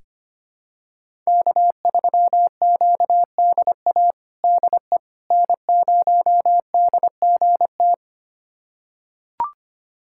Morse code, heard as a keyed tone, about 25 wpm, 700 Hz. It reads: K3QDA DE N0DGT
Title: USA Callsign DE Callsign - 5 Characters - 25wpm